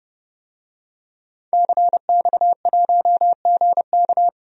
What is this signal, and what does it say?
Morse code, heard as a keyed tone, about 30 wpm, 700 Hz. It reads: CX1GK
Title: DXCC-100 Call Signs - 5 Characters - 30wpm